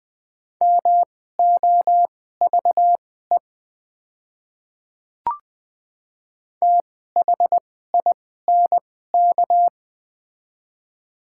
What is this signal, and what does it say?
Morse code, heard as a keyed tone, about 20 wpm, spaced as for 14 wpm, 700 Hz. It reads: MOVE THINK
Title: 2x Character Spacing - Top 200 Words - 20wpm